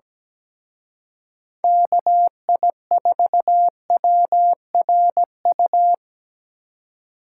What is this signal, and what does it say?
Morse code, heard as a keyed tone, about 21 wpm, 700 Hz. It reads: KI4WRU
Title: Speed Racing - USA 2x3 Call Signs 17wpm